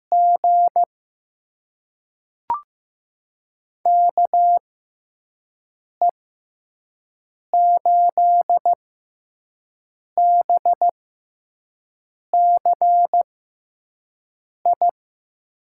Morse code, heard as a keyed tone, about 15 wpm, spaced as for 5 wpm, 700 Hz. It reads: G KE8BCI
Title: USA Call Signs - 6 Characters - 6x Character Spacing - 15wpm